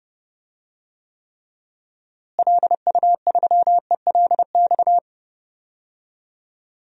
Morse code, heard as a keyed tone, about 32 wpm, 700 Hz. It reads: LU3ELX
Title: DXCC-100 Call Signs - 6 Characters - Repeated 3x - 30wpm